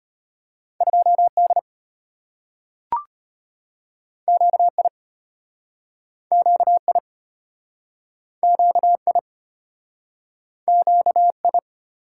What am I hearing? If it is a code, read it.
Morse code, heard as a keyed tone, about 34 wpm, 700 Hz. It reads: 2D QI QI QI QI